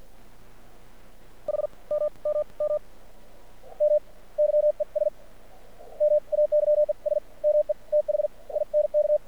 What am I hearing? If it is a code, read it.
Morse code, heard as a keyed tone, about 53 wpm, spaced as for 31 wpm, 600 Hz. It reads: HMMMM YES MAYES ME THINK
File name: archived / CRSnet